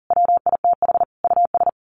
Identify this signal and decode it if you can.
Morse code, heard as a keyed tone, about 40 wpm, 700 Hz. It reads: WITH US
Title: Sets of 5 Words Encore 40wpm